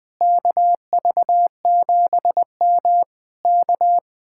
Morse code, heard as a keyed tone, about 20 wpm, 700 Hz. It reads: KV7M K
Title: Speed-Racing Common QSO Phrases - 20wpm